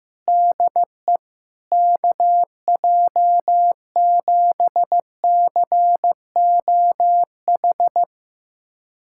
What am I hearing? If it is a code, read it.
Morse code, heard as a keyed tone, about 15 wpm, 700 Hz. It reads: DE KJ7COH
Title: USA Callsign DE Callsign 15wpm